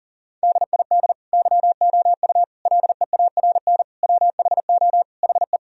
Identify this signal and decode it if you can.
Morse code, heard as a keyed tone, about 40 wpm, 700 Hz. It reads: DID YOU LEARN WHO HE
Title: Sentences from Top 1000 Words - Review Part 1 - 40wpm